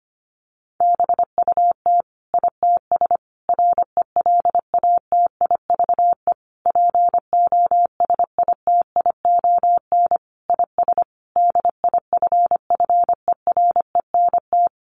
Morse code, heard as a keyed tone, about 25 wpm, 700 Hz. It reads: BUT ITS RELATIVE POSITION IS DIFFERENT
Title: Sentences from Top 1400 Words 25wpm